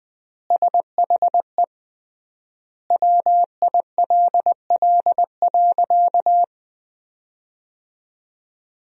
Morse code, heard as a keyed tone, about 21 wpm, 700 Hz. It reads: SHE WILL.
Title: 3x Word-Spacing - Sentences from Top 500 Words - 20wpm